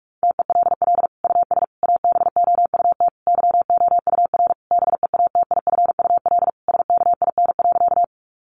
Morse code, heard as a keyed tone, about 44 wpm, 700 Hz. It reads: TELL US ABOUT YOUR BEATIFUL SKIN.
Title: Sentences from Top 1300 Words 45wpm